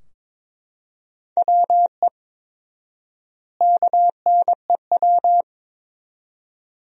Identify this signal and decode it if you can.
Morse code, heard as a keyed tone, about 22 wpm, 700 Hz. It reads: WE KNEW
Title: 4x Word-Spacing - Sentences from Top 400 Words - 22wpm